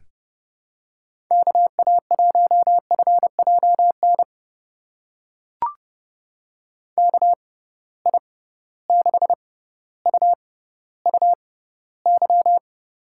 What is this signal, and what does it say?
Morse code, heard as a keyed tone, about 31 wpm, spaced as for 16 wpm, 700 Hz. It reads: KA1FJN KI6UUY
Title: USA Call Signs - 6 Characters - 6x Character Spacing - 30wpm